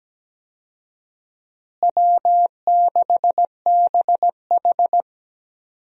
Morse code, heard as a keyed tone, about 17 wpm, 700 Hz. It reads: W6BH
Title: USA Call Signs - 4 Characters - Repeated 2x - 17wpm